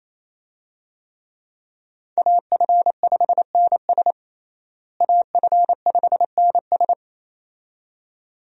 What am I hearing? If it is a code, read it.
Morse code, heard as a keyed tone, about 28 wpm, 700 Hz. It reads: AF5NS AF5NS
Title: USA Call Signs - 5 Characters - Repeated 2x - 28wpm